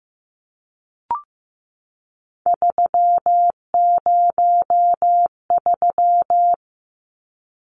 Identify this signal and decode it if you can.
Morse code, heard as a keyed tone, about 15 wpm, 700 Hz. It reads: 303
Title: Numbers - 3-Digit 15wpm